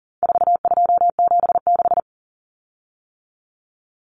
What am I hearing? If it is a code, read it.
Morse code, heard as a keyed tone, about 40 wpm, 700 Hz. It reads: 4276
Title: Common Cut Numbers - 4-Digit 40wpm